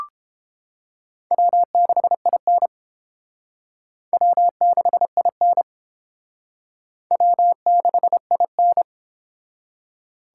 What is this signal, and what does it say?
Morse code, heard as a keyed tone, about 30 wpm, 700 Hz. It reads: W6IN W6IN W6IN